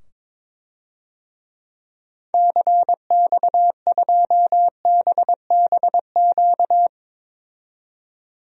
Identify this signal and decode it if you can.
Morse code, heard as a keyed tone, about 21 wpm, 700 Hz. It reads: CX2BBQ
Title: DXCC-100 Call Signs - 6 Characters - 22wpm